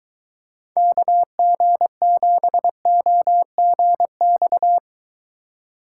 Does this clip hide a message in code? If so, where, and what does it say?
Yes: Morse code from 0.8 to 4.8 s, heard as a keyed tone, about 23 wpm, 700 Hz. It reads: KG7OGX